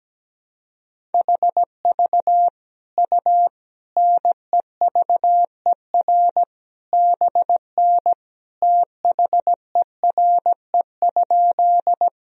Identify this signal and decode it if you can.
Morse code, heard as a keyed tone, about 17 wpm, 700 Hz. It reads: HV U NEVER BN THERE?
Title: Sentences from Top 300 Words and Abbreviations 17wpm